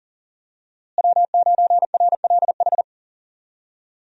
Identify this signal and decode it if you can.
Morse code, heard as a keyed tone, about 40 wpm, 700 Hz. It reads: W9RLH
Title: USA Call Signs - 5 Characters - Repeated 2x - 40wpm